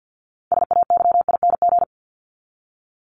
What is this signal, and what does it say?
Morse code, heard as a keyed tone, about 49 wpm, 700 Hz. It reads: SAYING
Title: ING Suffix - Top 100 Words with Base Words 50wpm